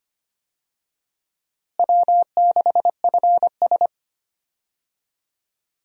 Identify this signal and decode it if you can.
Morse code, heard as a keyed tone, about 25 wpm, 700 Hz. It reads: W6FS
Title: USA Call Signs - 4 Characters - Repeated 2x - 25wpm